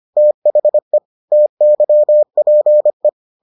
Morse code, heard as a keyed tone, about 25 wpm, 600 Hz. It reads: THE TYPE